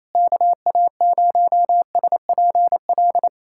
Morse code, heard as a keyed tone, about 28 wpm, 700 Hz. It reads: KA0SPL
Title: USA Callsign DE Callsign - 6 Characters - 28wpm